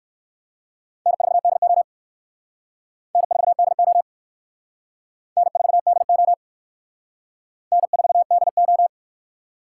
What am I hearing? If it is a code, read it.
Morse code, heard as a keyed tone, about 54 wpm, 700 Hz. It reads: N4DO N4DO N4DO N4DO
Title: Speed Racing - USA 1x2 Call Signs 45wpm